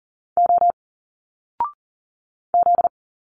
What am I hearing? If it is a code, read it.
Morse code, heard as a keyed tone, about 40 wpm, 700 Hz. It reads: O Z
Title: Single Letters 40wpm